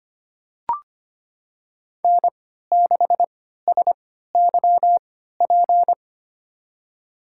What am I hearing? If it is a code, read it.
Morse code, heard as a keyed tone, about 25 wpm, spaced as for 14 wpm, 700 Hz. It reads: N6SYP
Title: USA Call Signs - 3x Character Spacing - 25wpm